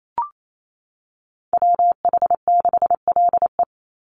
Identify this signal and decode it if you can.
Morse code, heard as a keyed tone, about 28 wpm, 700 Hz. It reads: WH6LE
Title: NA SOTA Activator Callsigns 28wpm